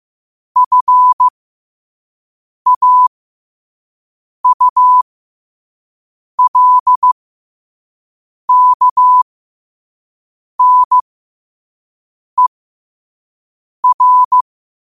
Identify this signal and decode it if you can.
Morse code, heard as a keyed tone, about 15 wpm, spaced as for 5 wpm, 1000 Hz. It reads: FAULKNER
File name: qotdfiles / 5wpm